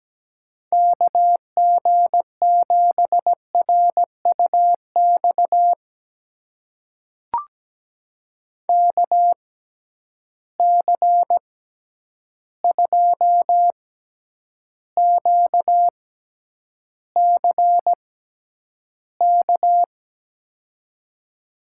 Morse code, heard as a keyed tone, about 17 wpm, spaced as for 9 wpm, 700 Hz. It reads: KG7RUX KC2QCK
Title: USA Call Signs - 6 Characters - 6x Character Spacing - 17wpm